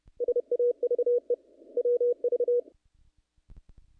Morse code, heard as a keyed tone, about 31 wpm, 480 Hz. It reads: SAVE WV
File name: contests / naqp-cw-k5tr-op-k5pi- -1900